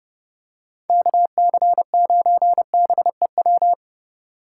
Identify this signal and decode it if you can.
Morse code, heard as a keyed tone, about 30 wpm, 700 Hz. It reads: KC9BEW